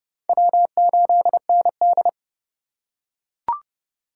Morse code, heard as a keyed tone, about 30 wpm, 700 Hz. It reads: W8ND
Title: USA Call Signs - 4 Characters - 2x Character Spacing - 30wpm